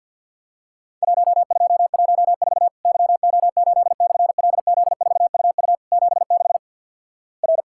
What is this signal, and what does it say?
Morse code, heard as a keyed tone, about 54 wpm, 700 Hz. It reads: 1214 YO8/LZ4UU 76 R
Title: Worked All Europe DX Contest - QTC Exchange - 50wpm